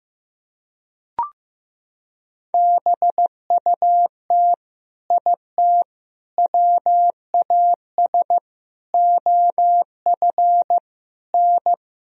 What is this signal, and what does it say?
Morse code, heard as a keyed tone, about 15 wpm, 700 Hz. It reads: BUT IT WAS OF N
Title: Sentences from Top 100 Words and Abbreviations 15wpm